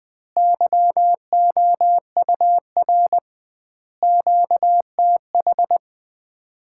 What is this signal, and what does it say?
Morse code, heard as a keyed tone, about 20 wpm, 700 Hz. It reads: YOUR QTH
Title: Common QSO Phrases - 2x Word-Spacing - 20wpm